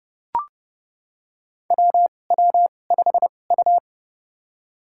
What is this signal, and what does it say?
Morse code, heard as a keyed tone, about 30 wpm, spaced as for 22 wpm, 700 Hz. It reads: WW5U
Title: USA Call Signs - 4 Characters - 2x Character Spacing - 30wpm